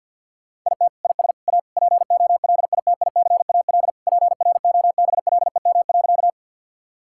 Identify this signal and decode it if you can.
Morse code, heard as a keyed tone, about 50 wpm, 700 Hz. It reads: IT IS A POLITICAL PROBLEM.
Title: Sentences from Top 1000 Words - Review Part 5 - 50wpm